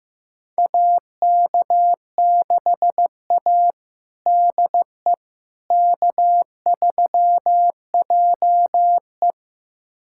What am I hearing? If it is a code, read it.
Morse code, heard as a keyed tone, about 16 wpm, 700 Hz. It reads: AK6A DE K3JE
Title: USA Callsign DE Callsign - 4 Characters - 15wpm